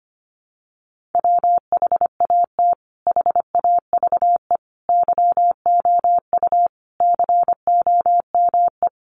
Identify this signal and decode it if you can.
Morse code, heard as a keyed tone, about 25 wpm, 700 Hz. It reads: WHAT HAVE YOU COME
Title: Sentences from Top 100 Words 25wpm